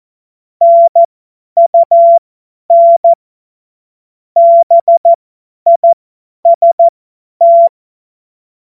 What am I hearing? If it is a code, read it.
Morse code, heard as a keyed tone, about 14 wpm, spaced as for 10 wpm, 680 Hz. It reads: NUN BIST
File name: Morsekurs / Witze / Witz 29. Juni 2021, 50 F70